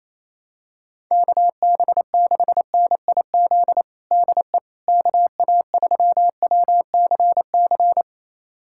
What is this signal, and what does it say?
Morse code, heard as a keyed tone, about 28 wpm, 700 Hz. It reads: KB6NIZ DE KA3WCC
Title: USA Callsign DE Callsign - 6 Characters - 28wpm